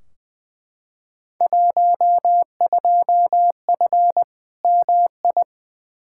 Morse code, heard as a keyed tone, about 20 wpm, 700 Hz. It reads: 12F MI
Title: Field Day Exchanges Encore - 20wpm